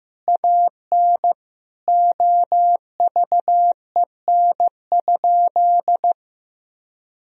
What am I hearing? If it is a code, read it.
Morse code, heard as a keyed tone, about 15 wpm, 700 Hz. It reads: AN OVEN?
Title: Sentences from Top 1000 Words - Review Part 4 - 15wpm